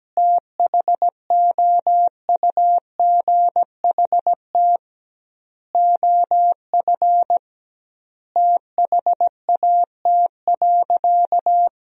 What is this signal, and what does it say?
Morse code, heard as a keyed tone, about 17 wpm, 700 Hz. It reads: THOUGHT OF THAT.
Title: 2x Word-Spacing - Sentences from Top 300 Words - 17wpm